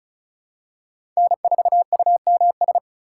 Morse code, heard as a keyed tone, about 36 wpm, 700 Hz. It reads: N4UMS
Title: USA Call Signs - 5 Characters - 4x Character Spacing - 35wpm